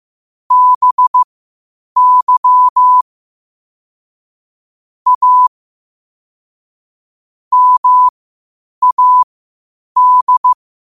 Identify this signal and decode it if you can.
Morse code, heard as a keyed tone, about 15 wpm, spaced as for 8 wpm, 1000 Hz. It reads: BY A MAD
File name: qotdfiles / 8wpm